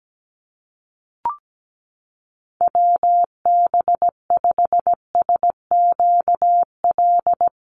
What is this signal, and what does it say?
Morse code, heard as a keyed tone, about 17 wpm, 700 Hz. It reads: WB5SQL
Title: USA Call Signs Repeated 3x 17wpm